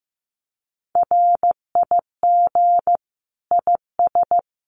Morse code, heard as a keyed tone, about 15 wpm, 700 Hz. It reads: RIG IS
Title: Common QSO Phrases Encore 15wpm